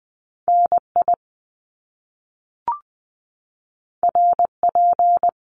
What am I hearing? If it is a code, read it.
Morse code, heard as a keyed tone, about 20 wpm, 700 Hz. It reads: NI RP
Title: Top Two Letter Combinations in Words 20wpm